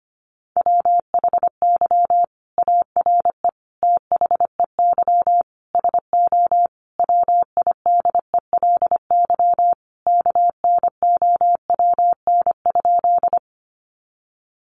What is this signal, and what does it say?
Morse code, heard as a keyed tone, about 25 wpm, 700 Hz. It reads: WHY ARE THEY SO WIDELY KNOWN?
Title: Sentences from Top 1200 Words 25wpm